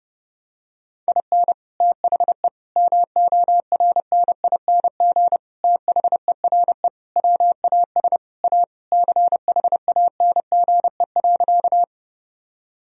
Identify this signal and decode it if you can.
Morse code, heard as a keyed tone, about 32 wpm, 700 Hz. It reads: IN THE MORNING THERE WAS A CHANGE.
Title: Sentences from Top 1000 Words - Review Part 1 - 30wpm